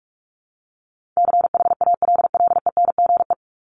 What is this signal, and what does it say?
Morse code, heard as a keyed tone, about 44 wpm, 700 Hz. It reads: CHALLENGE
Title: Top 751-1000 Words 45wpm